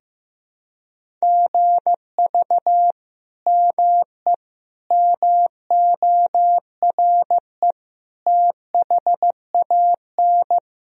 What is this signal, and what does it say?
Morse code, heard as a keyed tone, about 15 wpm, 700 Hz. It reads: GV ME MORE THAN
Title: Sentences from Top 200 Words and Abbreviations 15wpm